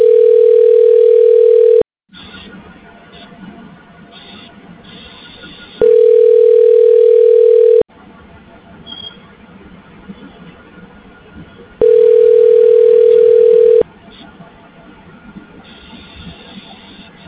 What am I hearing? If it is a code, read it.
Ringback tone (North America), 440 + 480 Hz.